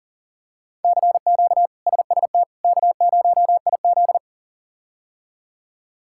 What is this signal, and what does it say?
Morse code, heard as a keyed tone, about 37 wpm, 700 Hz. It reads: CQ SST K0IZ
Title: K1USN SST QSO Practice - 40wpm